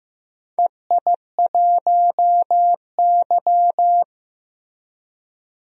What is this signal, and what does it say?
Morse code, heard as a keyed tone, about 15 wpm, 700 Hz. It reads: EI1Y
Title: DXCC-100 Call Signs - 4 Characters - Repeated 2x - 15wpm